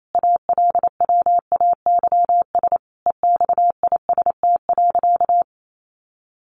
Morse code, heard as a keyed tone, about 28 wpm, 700 Hz. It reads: ALWAYS EXIST.
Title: Sentences from Top 1500 Words 28wpm